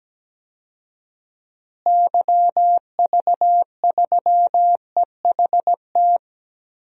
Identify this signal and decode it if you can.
Morse code, heard as a keyed tone, about 17 wpm, 700 Hz. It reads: YV3EHT
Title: DXCC-100 Call Signs - 6 Characters - Repeated 3x - 17wpm